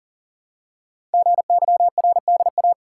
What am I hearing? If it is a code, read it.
Morse code, heard as a keyed tone, about 42 wpm, 700 Hz. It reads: GYRDA
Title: Code Groups - 5 Letters - 40wpm